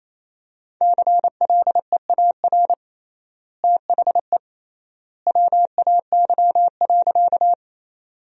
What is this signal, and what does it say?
Morse code, heard as a keyed tone, about 28 wpm, 700 Hz. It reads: CLEAR THE WAY.
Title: 3x Word-Spacing - Sentences from Top 500 Words - 28wpm